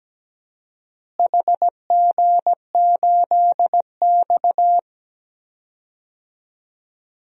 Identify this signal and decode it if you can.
Morse code, heard as a keyed tone, about 17 wpm, 700 Hz. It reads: HG8X
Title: DXCC-100 Call Signs - 4 Characters - Repeated 3x - 17wpm